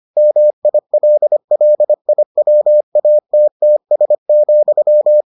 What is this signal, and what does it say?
Morse code, heard as a keyed tone, about 25 wpm, 600 Hz. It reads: MILLIWATTS,